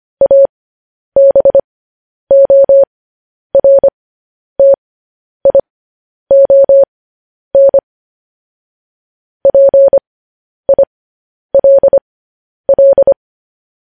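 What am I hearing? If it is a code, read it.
Morse code, heard as a keyed tone, about 25 wpm, spaced as for 10 wpm, 550 Hz. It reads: ABORTION PILL
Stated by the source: Morse code news headlines practice at 10 wpm.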